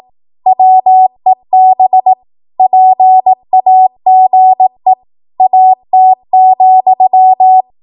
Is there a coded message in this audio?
Yes: Morse code from 0.5 to 7.6 s, heard as a keyed tone, about 18 wpm, 750 Hz. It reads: WEB PAGE AT,